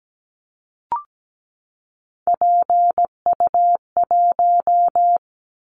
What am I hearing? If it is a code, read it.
Morse code, heard as a keyed tone, about 17 wpm, 700 Hz. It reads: PU1
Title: Callsign Prefixes for Top 100 DXCC - 17wpm